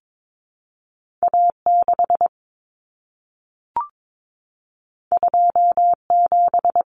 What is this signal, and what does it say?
Morse code, heard as a keyed tone, about 22 wpm, 700 Hz. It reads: A6 27
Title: Common Cut Numbers - 2-Digit 22wpm